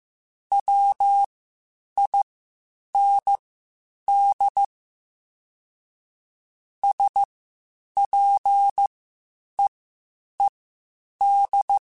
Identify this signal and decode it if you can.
Morse code, heard as a keyed tone, about 15 wpm, spaced as for 8 wpm, 800 Hz. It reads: WIND SPEED
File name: Spaced Characters / Spaced Characters-mp3s / Weather5 22 sp @13